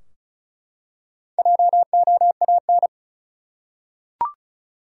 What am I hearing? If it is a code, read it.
Morse code, heard as a keyed tone, about 35 wpm, 700 Hz. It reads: JOAN
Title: USA Common Names - Max 5 Letters - 35wpm